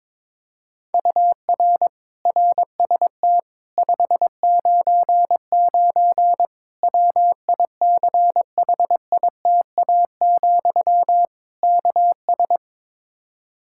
Morse code, heard as a keyed tone, about 21 wpm, 700 Hz. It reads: UR RST 599 WICHITA, KS
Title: Practice QSO - Exchange 1 - Beginning - 22wpm